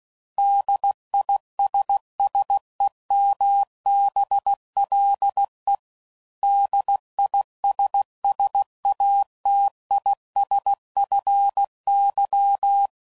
Morse code, heard as a keyed tone, about 16 wpm, 800 Hz. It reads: DISSEMBLE DISSATISFY